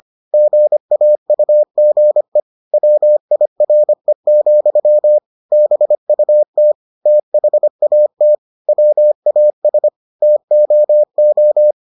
Morse code, heard as a keyed tone, about 25 wpm, 600 Hz. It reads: GAUGE WIRE, BUT THAT WAS TOO